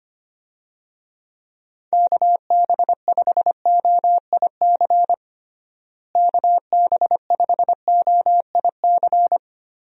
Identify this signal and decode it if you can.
Morse code, heard as a keyed tone, about 25 wpm, 700 Hz. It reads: KB5OIC KB5OIC
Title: USA Call Signs - 6 Characters - Repeated 2x - 25wpm